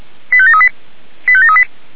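Five-tone selective call (ZVEI), heard as quick runs of five tones, tones of about 70 ms. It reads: 85738 85738
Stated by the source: ZVEI five-tone selective call received with an SDR.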